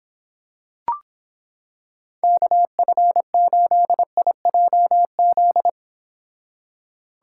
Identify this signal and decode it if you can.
Morse code, heard as a keyed tone, about 26 wpm, 700 Hz. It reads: KF8IJZ